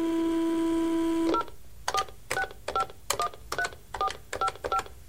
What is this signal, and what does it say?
DTMF, heard as telephone touch tones, tones of about 90 ms. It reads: 046513455